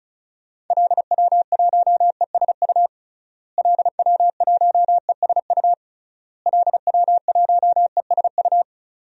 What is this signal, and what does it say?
Morse code, heard as a keyed tone, about 34 wpm, 700 Hz. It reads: LW1ESU LW1ESU LW1ESU
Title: DXCC-100 Call Signs - 6 Characters - Repeated 3x - 35wpm